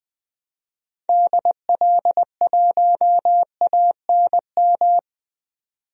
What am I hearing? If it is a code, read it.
Morse code, heard as a keyed tone, about 20 wpm, 700 Hz. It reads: DL1ANM
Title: DXCC-100 Call Signs - 6 Characters - 20wpm